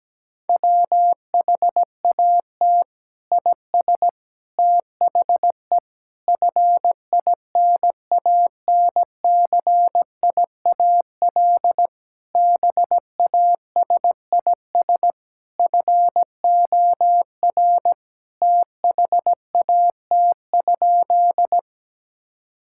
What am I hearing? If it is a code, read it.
Morse code, heard as a keyed tone, about 17 wpm, 700 Hz. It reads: WHAT IS THE FINANCIAL BASIS FOR THAT?
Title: Sentences from Top 1000 Words - Review Part 5 - 17wpm